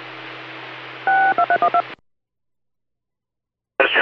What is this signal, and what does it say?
DTMF, heard as telephone touch tones, tones of about 80 ms. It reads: 62312